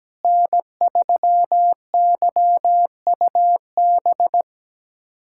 Morse code, heard as a keyed tone, about 17 wpm, 700 Hz. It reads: N3YUB
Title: USA Callsign DE Callsign - 5 Characters - 17wpm